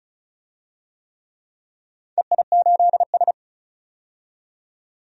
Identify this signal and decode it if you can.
Morse code, heard as a keyed tone, about 34 wpm, 700 Hz. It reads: EI8S
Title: DXCC-100 Call Signs - 4 Characters - 35wpm